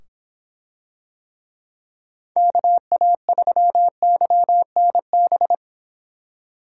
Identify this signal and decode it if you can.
Morse code, heard as a keyed tone, about 23 wpm, 700 Hz. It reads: KA3YNB